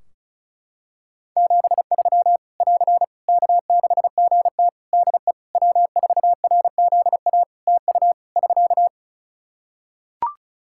Morse code, heard as a keyed tone, about 36 wpm, 700 Hz. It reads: 73 <AR> K6GT DE W4RZA TU <SK>
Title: Practice QSO - Exchange 4 - Ending - 35wpm